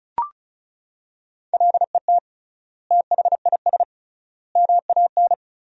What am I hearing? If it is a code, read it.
Morse code, heard as a keyed tone, about 36 wpm, 700 Hz. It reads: LET THIS MAN
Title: 3x Word-Spacing - Sentences from Top 400 Words - 35wpm